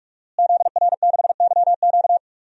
Morse code, heard as a keyed tone, about 46 wpm, 700 Hz. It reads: ZR6YQ
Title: DXCC-100 Call Signs - 5 Characters - Repeated 3x - 45wpm